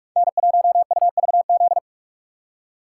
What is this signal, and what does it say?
Morse code, heard as a keyed tone, about 41 wpm, 700 Hz. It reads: N1UVZ
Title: USA Call Signs - 5 Characters - 3x Character Spacing - 45wpm